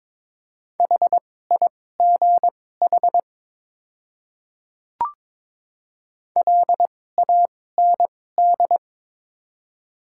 Morse code, heard as a keyed tone, about 23 wpm, spaced as for 16 wpm, 700 Hz. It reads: HIGH LAND
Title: 2x Character Spacing - Top 500 Words - 22wpm